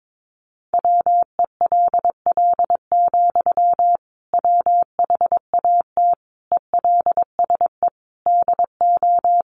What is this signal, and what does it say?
Morse code, heard as a keyed tone, about 22 wpm, 700 Hz. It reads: WELL, WHAT ELSE DO
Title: Sentences from Top 800 Words 22wpm